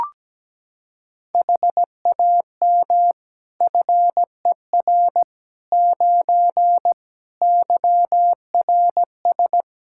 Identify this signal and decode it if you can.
Morse code, heard as a keyed tone, about 17 wpm, 700 Hz. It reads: HAM FER 9 YRS
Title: K1USN SST QSO Practice 17wpm